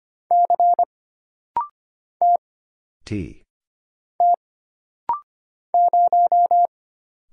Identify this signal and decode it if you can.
Morse code, heard as a keyed tone, about 25 wpm, 700 Hz. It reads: C T T 0
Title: Single Letter-Number - Rapid-Fire 25wpm